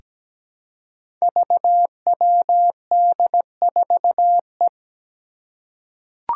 Morse code, heard as a keyed tone, about 17 wpm, 700 Hz. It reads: VWD4E
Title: Five Challenging Letter-Number Combinations 17wpm